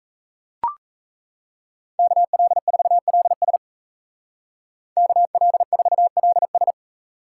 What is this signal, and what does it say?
Morse code, heard as a keyed tone, about 33 wpm, 700 Hz. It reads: KL4LS KL4LS
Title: Speed Racing - USA 2x2 Call Signs 28wpm